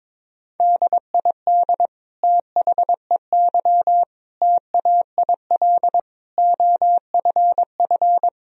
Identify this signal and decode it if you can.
Morse code, heard as a keyed tone, about 22 wpm, 700 Hz. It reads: DID THEY TAIL OFF
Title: Sentences from Top 1000 Words - Review Part 3 - 22wpm